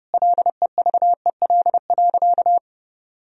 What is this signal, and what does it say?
Morse code, heard as a keyed tone, about 30 wpm, 700 Hz. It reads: LEVEL.
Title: Sentences from Top 1000 Words - Review Part 4 - 30wpm